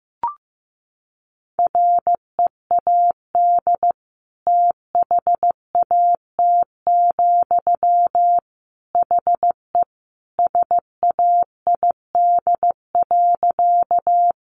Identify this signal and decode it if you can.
Morse code, heard as a keyed tone, about 15 wpm, 700 Hz. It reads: READ THAT, HE SAID.
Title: Sentences from Top 200 Words 15wpm